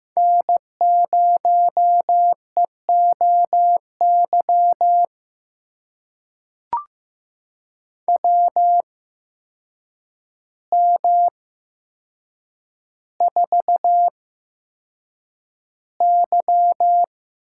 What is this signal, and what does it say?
Morse code, heard as a keyed tone, about 15 wpm, spaced as for 7 wpm, 700 Hz. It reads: N0EOY WM4Y
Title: USA Call Signs - 8x Character Spacing - 15wpm